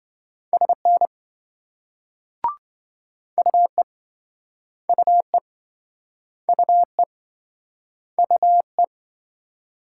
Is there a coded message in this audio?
Yes: Morse code from 0.5 to 8.8 s, heard as a keyed tone, about 26 wpm, 700 Hz. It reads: SN UE UE UE UE